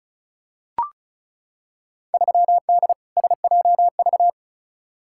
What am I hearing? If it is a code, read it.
Morse code, heard as a keyed tone, about 34 wpm, 700 Hz. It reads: 3D SJV
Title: Field Day Exchanges - 35wpm